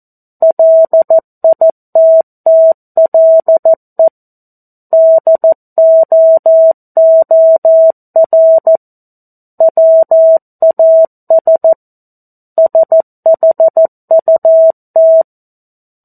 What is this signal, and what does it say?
Morse code, heard as a keyed tone, about 14 wpm, 650 Hz. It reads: LITTLE DOOR WAS SHUT
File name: 02-Chapter-02 14 WPM